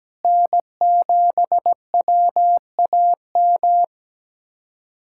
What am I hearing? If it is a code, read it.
Morse code, heard as a keyed tone, about 17 wpm, 700 Hz. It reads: N7WAM
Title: USA Callsign DE Callsign - 5 Characters - 17wpm